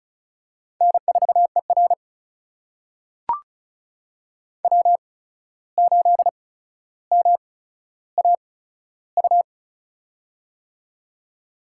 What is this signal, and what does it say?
Morse code, heard as a keyed tone, about 34 wpm, spaced as for 14 wpm, 700 Hz. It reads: N4ER W8MAU
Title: USA Call Signs - 8x Character Spacing - 35wpm